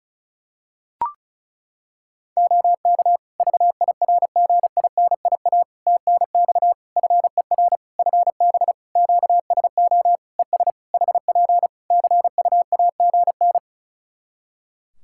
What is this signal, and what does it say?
Morse code, heard as a keyed tone, about 34 wpm, 700 Hz. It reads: OK VIRGINIA TNX FER FB QSO ES HP CUAGN
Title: Practice QSO - Exchange 4 - Ending - 35wpm